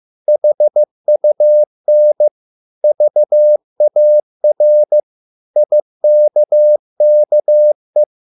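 Morse code, heard as a keyed tone, about 15 wpm, 600 Hz. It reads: HUN VAR IKKE
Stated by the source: Morse code at a slow 15 wpm from CW med Gnister.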